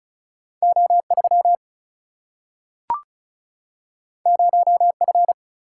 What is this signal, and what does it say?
Morse code, heard as a keyed tone, about 36 wpm, 700 Hz. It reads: O3 0F
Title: Single-Letter Single-Number 35wpm